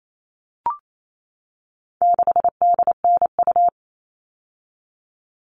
Morse code, heard as a keyed tone, about 28 wpm, 700 Hz. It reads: 6DNU
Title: All Cut Numbers - 4-Digit 28wpm